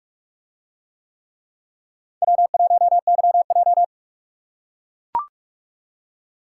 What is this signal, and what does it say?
Morse code, heard as a keyed tone, about 41 wpm, 700 Hz. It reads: W1YJ